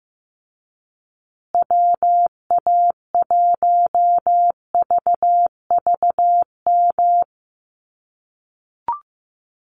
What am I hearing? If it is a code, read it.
Morse code, heard as a keyed tone, about 15 wpm, 700 Hz. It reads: WA1VVM
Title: USA Call Signs Repeated 2x 15wpm